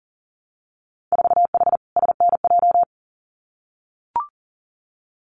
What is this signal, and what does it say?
Morse code, heard as a keyed tone, about 42 wpm, 700 Hz. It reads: 4H SNJ
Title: Winter Field Day Exchanges Encore 40wpm